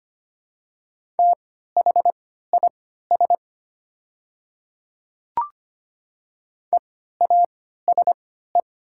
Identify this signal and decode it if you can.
Morse code, heard as a keyed tone, about 25 wpm, spaced as for 14 wpm, 700 Hz. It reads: THIS EASE
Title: 3x Character Spacing - Top 300 Words - 25wpm